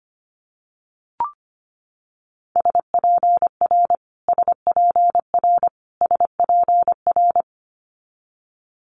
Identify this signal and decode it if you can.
Morse code, heard as a keyed tone, about 25 wpm, 700 Hz. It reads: SPR SPR SPR
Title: Multi-letter Phonemes Repeated 3x 25wpm